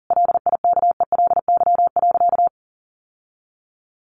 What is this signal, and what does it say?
Morse code, heard as a keyed tone, about 40 wpm, 700 Hz. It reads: LIKELY.
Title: Sentences from Top 600 Words 40wpm